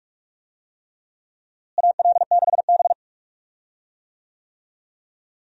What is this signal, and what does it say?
Morse code, heard as a keyed tone, about 46 wpm, 700 Hz. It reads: AL6B
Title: USA Call Signs - 4 Characters - Repeated 3x - 45wpm